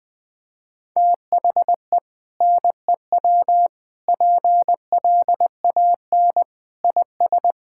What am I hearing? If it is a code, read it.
Morse code, heard as a keyed tone, about 21 wpm, 700 Hz. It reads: THE NEW PLAN IS
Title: Sentences with 2-Word Phrases - 20wpm